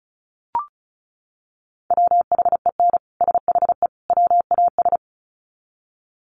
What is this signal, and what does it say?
Morse code, heard as a keyed tone, about 34 wpm, 700 Hz. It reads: WHEN SHE WAS
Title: Sets of 3 Words 35wpm